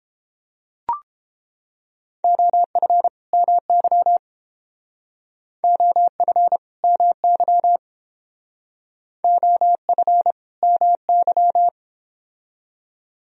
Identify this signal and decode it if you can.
Morse code, heard as a keyed tone, about 30 wpm, 700 Hz. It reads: OF MY OF MY OF MY